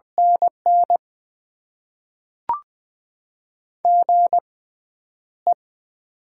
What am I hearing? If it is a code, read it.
Morse code, heard as a keyed tone, about 20 wpm, spaced as for 10 wpm, 700 Hz. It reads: NN GE
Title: Top 20 QSO Elements - 6x Character Spacing - 20wpm